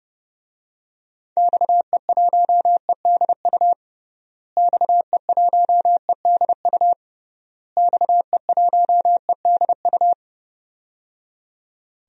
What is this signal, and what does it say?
Morse code, heard as a keyed tone, about 29 wpm, 700 Hz. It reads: XE1EDU XE1EDU XE1EDU
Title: DXCC-100 Call Signs - 6 Characters - Repeated 3x - 30wpm